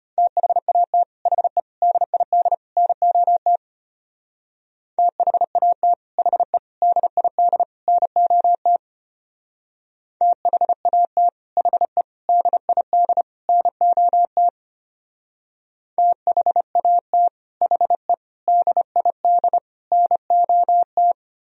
Morse code, heard as a keyed tone, about 33 wpm, 700 Hz. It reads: THAT HE DID NOT THAT HE DID NOT THAT HE DID NOT THAT HE DID NOT